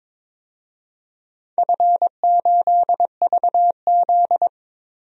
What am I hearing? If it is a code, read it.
Morse code, heard as a keyed tone, about 22 wpm, 700 Hz. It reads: F8VZ
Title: DXCC-100 Call Signs - 4 Characters - 22wpm